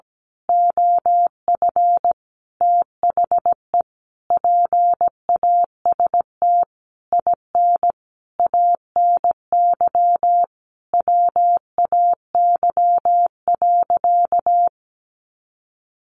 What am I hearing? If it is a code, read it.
Morse code, heard as a keyed tone, about 17 wpm, 700 Hz. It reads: OF THE PAST IN ANY WAY.
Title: A Sound Of Thunder 17wpm